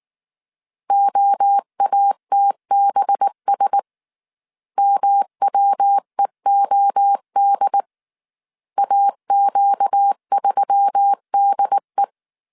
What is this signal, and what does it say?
Morse code, heard as a keyed tone, about 19 wpm, 800 Hz. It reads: OATBS MWEOD AQ3DE
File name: CW / DET / 1 6 1500C S C 18wpm 2356